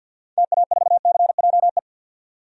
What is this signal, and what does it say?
Morse code, heard as a keyed tone, about 45 wpm, 700 Hz. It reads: TA4CJE
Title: DXCC-100 Call Signs - 6 Characters - 50wpm